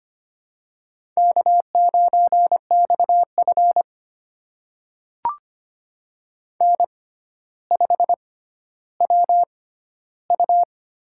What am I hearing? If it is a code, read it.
Morse code, heard as a keyed tone, about 25 wpm, spaced as for 13 wpm, 700 Hz. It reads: K9XF N5WU
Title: USA Call Signs - 4 Characters - 6x Character Spacing - 25wpm